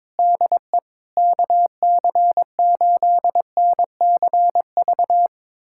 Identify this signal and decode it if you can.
Morse code, heard as a keyed tone, about 22 wpm, 700 Hz. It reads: DE KC8NCV
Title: USA Callsign DE Callsign - 6 Characters - 22wpm